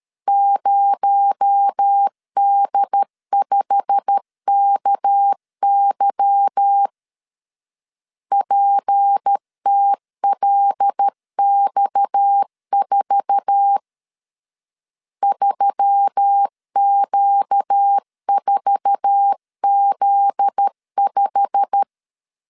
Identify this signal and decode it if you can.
Morse code, heard as a keyed tone, about 13 wpm, 800 Hz. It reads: 0D5KY PTLX4 3Q4Z5
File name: LEZIONI / 8 YJ45 / 8 YJ45 12wpm 2151